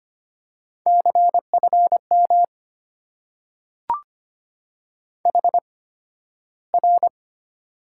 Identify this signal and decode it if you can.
Morse code, heard as a keyed tone, about 25 wpm, spaced as for 13 wpm, 700 Hz. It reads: CFM HR
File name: Top 90 QSO Elements - 8x Character Spacing - 25wpm